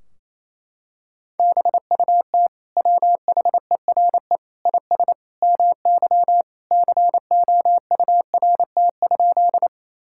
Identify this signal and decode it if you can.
Morse code, heard as a keyed tone, about 28 wpm, 700 Hz. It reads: BUT WHERE IS MY COURT?
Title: Sentences from Top 1600 Words 28wpm